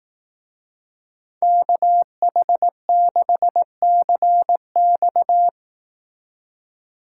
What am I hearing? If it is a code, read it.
Morse code, heard as a keyed tone, about 18 wpm, 700 Hz. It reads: KH6CX
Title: Speed Racing - USA 2x2 Call Signs 15wpm